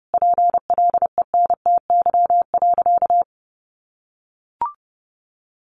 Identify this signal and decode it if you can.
Morse code, heard as a keyed tone, about 30 wpm, 700 Hz. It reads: PLENTY.
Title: Sentences from Top 1300 Words 30wpm